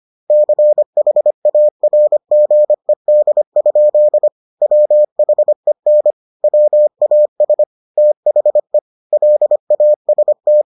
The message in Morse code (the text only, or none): CHARGED? WHEN WAS THE LAST